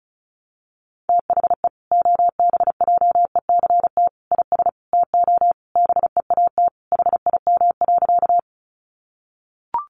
Morse code, heard as a keyed tone, about 34 wpm, 700 Hz. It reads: THE OBJECT IS TO BEAT HIM.
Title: Sentences from Top 1300 Words 35wpm